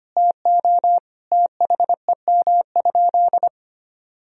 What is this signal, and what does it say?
Morse code, heard as a keyed tone, about 25 wpm, 700 Hz. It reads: TO THEM?
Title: Sentences from Top 1000 Words - Review Part 2 - 25wpm